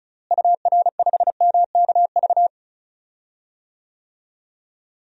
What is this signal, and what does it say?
Morse code, heard as a keyed tone, about 36 wpm, 700 Hz. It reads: UR5MKV
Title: DXCC-100 Call Signs - 6 Characters - Repeated 2x - 35wpm